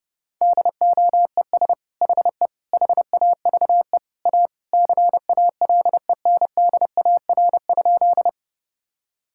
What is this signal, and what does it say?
Morse code, heard as a keyed tone, about 30 wpm, 700 Hz. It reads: DOES HE HAVE A CALENDAR?
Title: Sentences from Top 1600 Words 30wpm